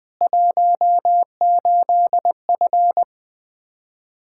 Morse code, heard as a keyed tone, about 20 wpm, 700 Hz. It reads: 18F
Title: Practice QSO - Exchange 2 - Info - 20wpm